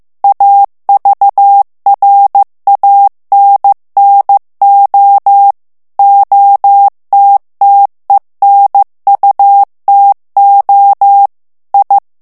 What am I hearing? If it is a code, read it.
Morse code, heard as a keyed tone, about 15 wpm, 800 Hz. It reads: AVRANNO OTTENUTO I
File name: morse mp3 / ITALIEN texte clair 2 vitesse 14 mots minute